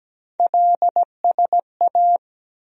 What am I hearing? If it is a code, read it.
Morse code, heard as a keyed tone, about 17 wpm, 700 Hz. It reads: LSA
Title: USA Call Sign Postfixes - 3 Letters 17wpm